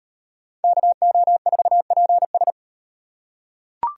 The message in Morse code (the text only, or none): KO4PS